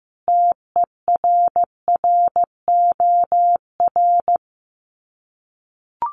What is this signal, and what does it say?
Morse code, heard as a keyed tone, about 15 wpm, 700 Hz. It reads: TERROR
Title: Sets of 5 Words Encore2 15wpm